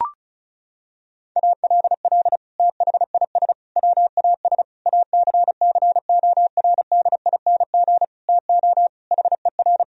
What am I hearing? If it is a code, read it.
Morse code, heard as a keyed tone, about 34 wpm, 700 Hz. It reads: ALL THIS WAS ACCORDING TO HER
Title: Sentences from Top 1000 Words - Review Part 4 - 35wpm